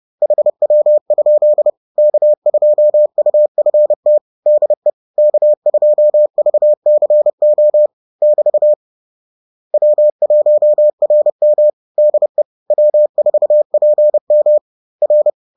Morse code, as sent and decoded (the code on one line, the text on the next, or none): .... .-- ..--.. / -.- ..--- ..- ..-. - / -.. / -.- ..--- ...- -.-. --- / -...- / .-- .---- .-. -- / -.. / .-- ....- .--. -- / .-.
HW? K2UFT DE K2VCO <BT> W1RM DE W4PM R